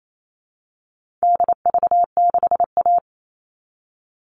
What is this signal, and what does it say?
Morse code, heard as a keyed tone, about 28 wpm, 700 Hz. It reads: DV6A
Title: All Cut Numbers - 4-Digit 28wpm